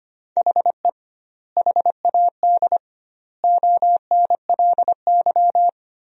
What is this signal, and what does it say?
Morse code, heard as a keyed tone, about 25 wpm, 700 Hz. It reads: HE HAD ONLY
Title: 2x Word-Spacing - Sentences from Top 300 Words - 25wpm